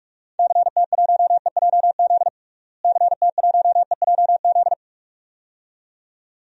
Morse code, heard as a keyed tone, about 42 wpm, 700 Hz. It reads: CT1EJZ CT1EJZ
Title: DXCC-100 Call Signs - 6 Characters - Repeated 2x - 45wpm